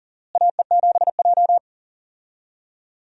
Morse code, heard as a keyed tone, about 40 wpm, 700 Hz. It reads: AE7J
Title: USA Call Signs - 4 Characters - Repeated 2x - 40wpm